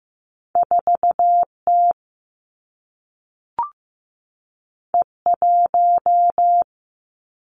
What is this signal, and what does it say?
Morse code, heard as a keyed tone, about 15 wpm, 700 Hz. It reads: 4 T E 1
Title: Two Challenging Letter-Number Combinations 15wpm